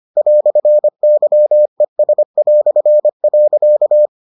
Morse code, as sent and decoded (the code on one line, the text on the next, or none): .-..-. -.-- . ... .-..-. .-.-.-
"YES".